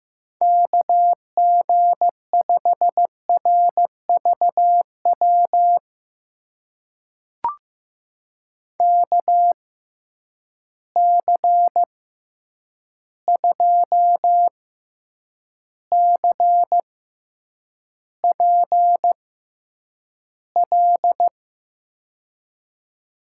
Morse code, heard as a keyed tone, about 15 wpm, spaced as for 8 wpm, 700 Hz. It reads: KG5RVW KC2CPL
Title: USA Call Signs - 6 Characters - 6x Character Spacing - 15wpm